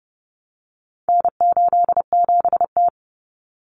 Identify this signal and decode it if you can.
Morse code, heard as a keyed tone, about 30 wpm, 700 Hz. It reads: N87T
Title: Common Cut Numbers - 4-Digit 30wpm